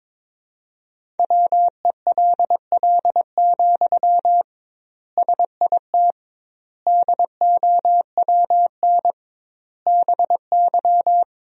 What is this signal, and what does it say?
Morse code, heard as a keyed tone, about 22 wpm, 700 Hz. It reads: WELL, SIT DOWN BY
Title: 2x Word-Spacing - Sentences from Top 500 Words - 22wpm